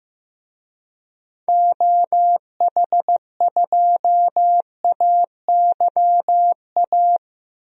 Morse code, heard as a keyed tone, about 15 wpm, 700 Hz. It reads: OH2AYA
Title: DXCC-100 Call Signs - 6 Characters - 15wpm